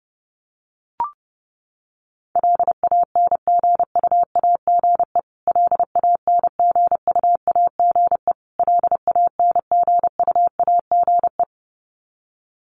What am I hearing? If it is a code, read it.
Morse code, heard as a keyed tone, about 30 wpm, 700 Hz. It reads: LANGUAGE LANGUAGE LANGUAGE
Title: Top 500 Words Repeated 30wpm